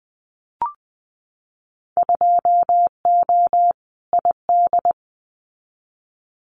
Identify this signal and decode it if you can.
Morse code, heard as a keyed tone, about 20 wpm, 700 Hz. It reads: 2O ID
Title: Winter Field Day Exchanges 20wpm